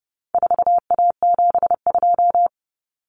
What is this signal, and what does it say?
Morse code, heard as a keyed tone, about 30 wpm, 700 Hz. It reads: 4A72
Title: Common Cut Numbers - 4-Digit 30wpm